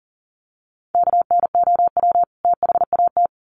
Morse code, heard as a keyed tone, about 40 wpm, 700 Hz. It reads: KNOW THAT
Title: Sets of 2 Words 40wpm